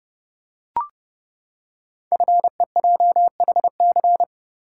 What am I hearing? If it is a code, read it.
Morse code, heard as a keyed tone, about 30 wpm, 700 Hz. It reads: FEJHC
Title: Code Groups - 5 Letters - 30wpm